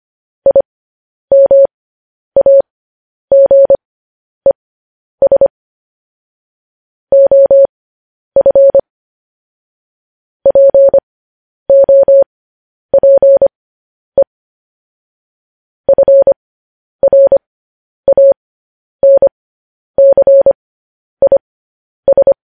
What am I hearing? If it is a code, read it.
Morse code, heard as a keyed tone, about 25 wpm, spaced as for 10 wpm, 550 Hz. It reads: IMAGES OF POPE FRANCIS
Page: news headlines in morse code at 10 wpm